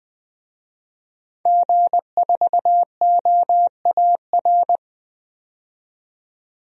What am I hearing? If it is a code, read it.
Morse code, heard as a keyed tone, about 20 wpm, 700 Hz. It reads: G4OAR
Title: DXCC-100 Call Signs - 5 Characters - 20wpm